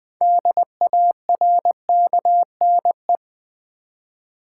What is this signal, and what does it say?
Morse code, heard as a keyed tone, about 20 wpm, 700 Hz. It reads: DARKNE
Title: Build A Word - 3 Letters At A Time - 8-Letter Words - 20wpm